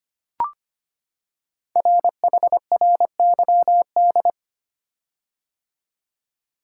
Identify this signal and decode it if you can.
Morse code, heard as a keyed tone, about 25 wpm, 700 Hz. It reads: RHRYD
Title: Code Groups - 5 Letters - 25wpm